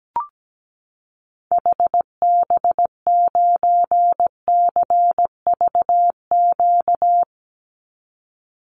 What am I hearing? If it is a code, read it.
Morse code, heard as a keyed tone, about 17 wpm, 700 Hz. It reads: HB9CVQ
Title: Call Signs Repeated 3x - 17wpm